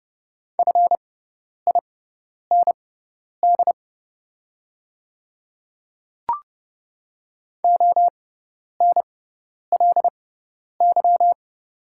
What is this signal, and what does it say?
Morse code, heard as a keyed tone, about 31 wpm, spaced as for 10 wpm, 690 Hz. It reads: FIND ONLY
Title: 6x Character Spacing - Top 200 Words - 30wpm